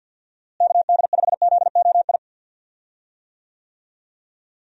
Morse code, heard as a keyed tone, about 47 wpm, 700 Hz. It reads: KD5ZOI
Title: USA Call Signs - 6 Characters - Repeated 3x - 50wpm